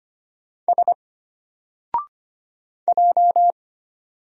Morse code, heard as a keyed tone, about 25 wpm, 700 Hz. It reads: S J
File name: Single-Letter-Number-25wpm